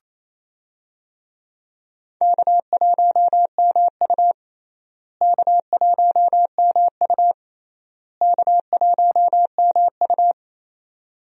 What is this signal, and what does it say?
Morse code, heard as a keyed tone, about 28 wpm, 700 Hz. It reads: K1MU K1MU K1MU
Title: USA Call Signs - 4 Characters - Repeated 3x - 28wpm